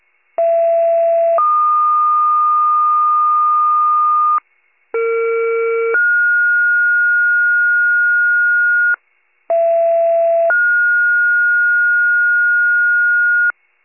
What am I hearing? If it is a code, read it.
Two-tone page, heard as a one-second tone, then a three-second tone, about 670 then 1220 Hz, about 460 then 1430 Hz, about 670 then 1430 Hz, with noise. It is heard from 0.4 to 13.5 s.
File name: pager / 39 RESCUE / wcalert